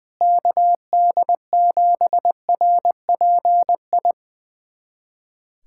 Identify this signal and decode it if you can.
Morse code, heard as a keyed tone, about 20 wpm, 700 Hz. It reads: KD7RPI
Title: USA Callsign DE Callsign 20wpm